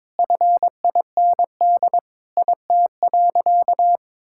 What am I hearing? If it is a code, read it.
Morse code, heard as a keyed tone, about 22 wpm, 700 Hz. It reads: FIND IT.
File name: practice / 22wpm / Sentences with 3-Word Phrases - 22wpm